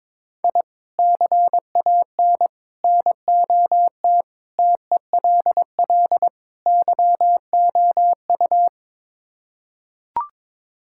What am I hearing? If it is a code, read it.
Morse code, heard as a keyed tone, about 22 wpm, 700 Hz. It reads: I CAN NOT TELL YOU
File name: practice / 22wpm / Build A Phrase - 5-Word Phrases - 22wpm